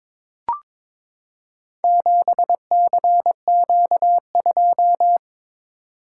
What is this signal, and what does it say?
Morse code, heard as a keyed tone, about 22 wpm, 700 Hz. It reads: 7CQ2
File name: practice / 22wpm / Four Letter-Number Combinations - 22wpm